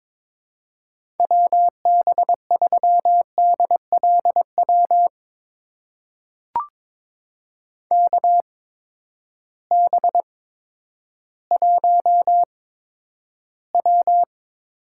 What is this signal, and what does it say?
Morse code, heard as a keyed tone, about 21 wpm, spaced as for 11 wpm, 700 Hz. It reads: WB3DLW KB1W
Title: USA Call Signs - 6 Characters - 8x Character Spacing - 22wpm